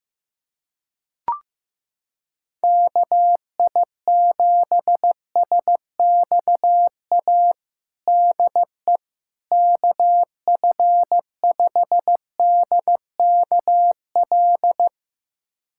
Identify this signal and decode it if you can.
Morse code, heard as a keyed tone, about 15 wpm, 700 Hz. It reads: KI7SXA DE KF5DKL
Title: USA Callsign DE Callsign - 6 Characters - 15wpm